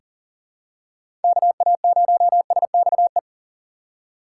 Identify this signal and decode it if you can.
Morse code, heard as a keyed tone, about 39 wpm, 700 Hz. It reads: KA0SXE
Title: USA Call Signs - 6 Characters - 2x Character Spacing - 40wpm